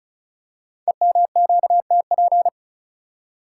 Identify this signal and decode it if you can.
Morse code, heard as a keyed tone, about 34 wpm, 690 Hz. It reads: EMQTP
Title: Code Groups - 5 Letters - 35wpm